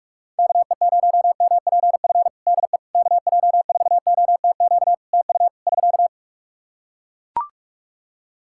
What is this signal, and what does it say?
Morse code, heard as a keyed tone, about 42 wpm, 700 Hz. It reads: KE0MPF DE KJ4OTQ TU <SK>
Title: Practice QSO - Exchange 4 - Ending - 45wpm